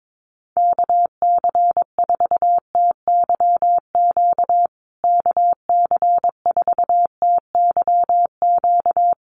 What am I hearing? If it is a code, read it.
Morse code, heard as a keyed tone, about 22 wpm, 700 Hz. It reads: KC4TYQ KC4TYQ
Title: USA Call Signs Repeated 3x 22wpm